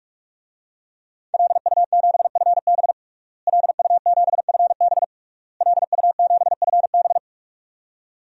Morse code, heard as a keyed tone, about 46 wpm, 700 Hz. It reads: LU7FB LU7FB LU7FB
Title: DXCC-100 Call Signs - 5 Characters - Repeated 3x - 45wpm